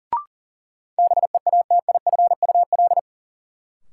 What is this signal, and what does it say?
Morse code, heard as a keyed tone, about 40 wpm, 700 Hz. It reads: BEATIFUL
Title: Sentences from Top 1300 Words - New Word Drill Alt - 40wpm